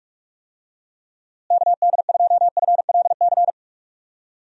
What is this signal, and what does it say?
Morse code, heard as a keyed tone, about 46 wpm, 700 Hz. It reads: KD2FLC
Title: USA Call Signs - 6 Characters - Repeated 2x - 45wpm